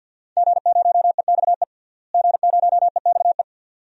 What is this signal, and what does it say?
Morse code, heard as a keyed tone, about 47 wpm, 700 Hz. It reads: G0EXE G0EXE
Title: DXCC-100 Call Signs - 5 Characters - Repeated 2x - 50wpm